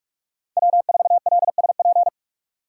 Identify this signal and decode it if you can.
Morse code, heard as a keyed tone, about 45 wpm, 690 Hz. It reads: W4LSP
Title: USA Call Signs - 5 Characters - 4x Character Spacing - 45wpm